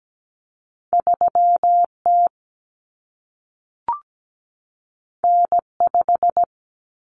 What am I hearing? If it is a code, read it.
Morse code, heard as a keyed tone, about 17 wpm, 700 Hz. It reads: 3T N5
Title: Common Cut Numbers - 2-Digit 17wpm